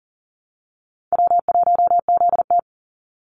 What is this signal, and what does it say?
Morse code, heard as a keyed tone, about 40 wpm, 700 Hz. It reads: W1ZT
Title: Easy Call Signs - 40wpm